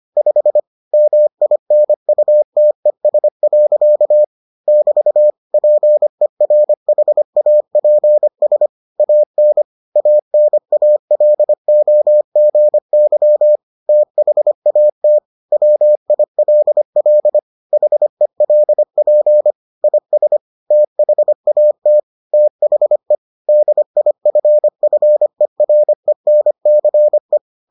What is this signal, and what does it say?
Morse code, heard as a keyed tone, about 25 wpm, 600 Hz. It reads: 5 MINUTES. <BT> PERHAPS AN ANALOGY THAT WILL HELP IS THAT THE DIFFERENCE